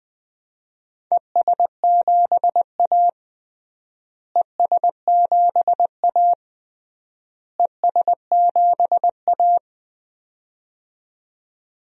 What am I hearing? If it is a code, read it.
Morse code, heard as a keyed tone, about 20 wpm, 700 Hz. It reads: ES7A ES7A ES7A
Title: DXCC-100 Call Signs - 4 Characters - Repeated 3x - 20wpm